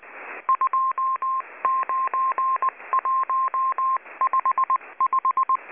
Morse code, heard as a keyed tone, about 20 wpm, 1060 Hz. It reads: 29155
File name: HM02- -0540utc